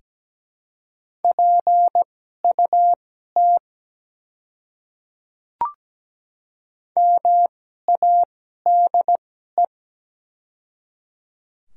Morse code, heard as a keyed tone, about 17 wpm, spaced as for 12 wpm, 700 Hz. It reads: PUT MADE
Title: 2x Character Spacing - Top 200 Words - 17wpm